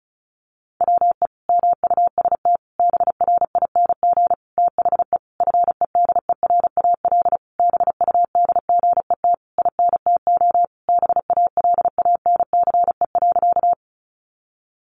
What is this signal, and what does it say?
Morse code, heard as a keyed tone, about 34 wpm, 700 Hz. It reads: WE MUST BRING THE FEDERAL BUDGET INTO BALANCE.
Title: Sentences from Top 1200 Words 35wpm